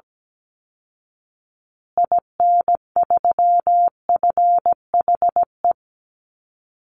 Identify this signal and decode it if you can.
Morse code, heard as a keyed tone, about 17 wpm, 700 Hz. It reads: IN3FHE
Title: Call Signs Repeated 2x - 17wpm